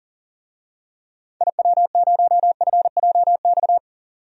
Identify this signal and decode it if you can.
Morse code, heard as a keyed tone, about 40 wpm, 700 Hz. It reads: IW0FJX
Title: DXCC-100 Call Signs - 6 Characters - 40wpm